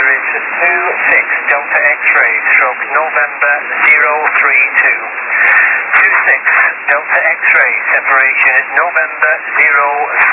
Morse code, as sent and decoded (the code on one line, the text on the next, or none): none